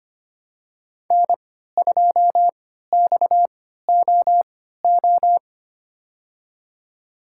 Morse code, heard as a keyed tone, about 25 wpm, spaced as for 14 wpm, 700 Hz. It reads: N2XOO
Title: USA Call Signs - 5 Characters - 3x Character Spacing - 25wpm